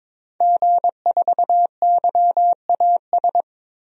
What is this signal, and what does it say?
Morse code, heard as a keyed tone, about 22 wpm, 700 Hz. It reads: G4YAS
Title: DXCC-100 Call Signs - 5 Characters - Repeated 2x - 22wpm